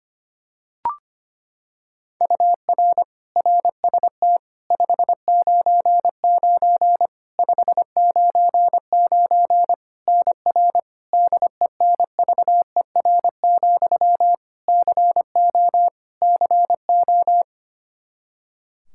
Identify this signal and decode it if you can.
Morse code, heard as a keyed tone, about 25 wpm, 700 Hz. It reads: UR RST 599 599 NR DENVER, CO CO
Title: Practice QSO - Exchange 1 - Beginning - 25wpm